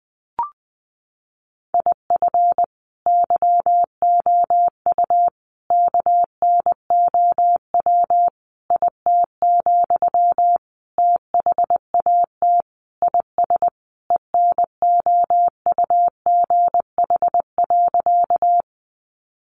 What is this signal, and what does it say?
Morse code, heard as a keyed tone, about 20 wpm, 700 Hz. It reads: IF YOU KNOW IT, THAT IS ENOUGH.
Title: Sentences from Top 400 Words 20wpm